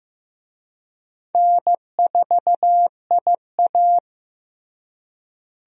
Morse code, heard as a keyed tone, about 15 wpm, 700 Hz. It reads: N4IA
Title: USA Call Signs - 4 Characters - Repeated 2x - 15wpm